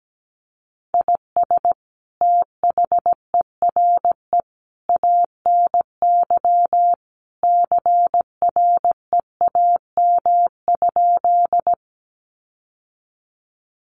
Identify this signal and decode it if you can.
Morse code, heard as a keyed tone, about 17 wpm, 700 Hz. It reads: IS THERE ANY CREAM?
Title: Sentences from Top 1500 Words 17wpm